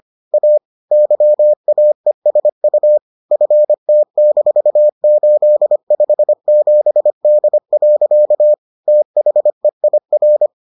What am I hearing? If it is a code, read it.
Morse code, heard as a keyed tone, about 25 wpm, 600 Hz. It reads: A YAESU FT-857D. THEIR